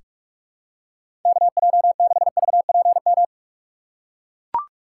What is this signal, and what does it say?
Morse code, heard as a keyed tone, about 43 wpm, 700 Hz. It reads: KJ6VPM
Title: USA Call Signs - 6 Characters - 3x Character Spacing - 45wpm